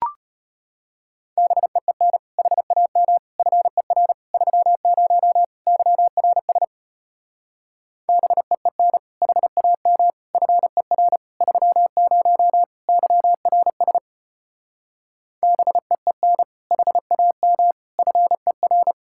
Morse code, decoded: BEEN HAM FER 30 YRS BEEN HAM FER 30 YRS BEEN HAM FER

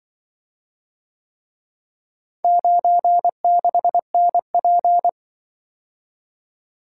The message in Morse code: 96NP

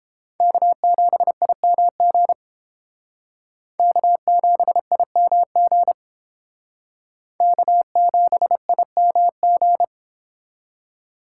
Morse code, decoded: K7IMG K7IMG K7IMG

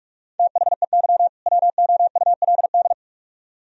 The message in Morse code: THEY WOULD

